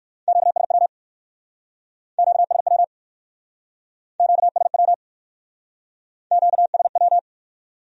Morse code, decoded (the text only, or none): QSW QSW QSW QSW